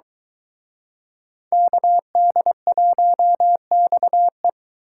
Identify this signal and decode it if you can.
Morse code, heard as a keyed tone, about 23 wpm, 700 Hz. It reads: KD1XE